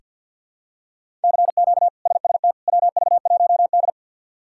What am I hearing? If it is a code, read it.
Morse code, heard as a keyed tone, about 47 wpm, 690 Hz. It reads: CQ SST WV1D
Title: K1USN SST QSO Practice - 50wpm